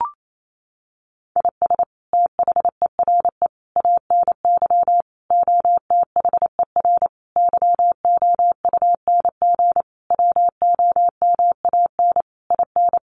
Morse code, heard as a keyed tone, about 28 wpm, 700 Hz. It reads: IS THERE ANY OTHER YOUNG WOMAN IN
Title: Sentences from Top 1100 Words 28wpm